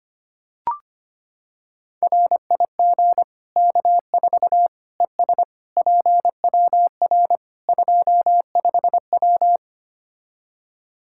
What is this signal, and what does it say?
Morse code, heard as a keyed tone, about 25 wpm, 700 Hz. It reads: RIG K4 ES PWR 25W
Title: Practice QSO - Exchange 2 - Info - 25wpm